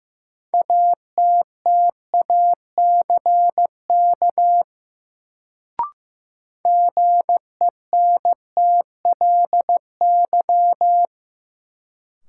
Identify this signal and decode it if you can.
Morse code, heard as a keyed tone, about 15 wpm, 700 Hz. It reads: ATTACK GENTLY
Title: Sentences from Top 1500 Words - New Word Drill - 15wpm